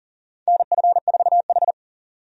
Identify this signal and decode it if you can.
Morse code, heard as a keyed tone, about 40 wpm, 700 Hz. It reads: NF4H
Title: USA Call Signs - 4 Characters - 40wpm